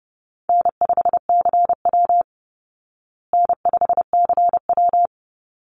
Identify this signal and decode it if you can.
Morse code, heard as a keyed tone, about 30 wpm, 700 Hz. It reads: N5CW N5CW
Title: Easy Call Signs Repeated 2x - 30wpm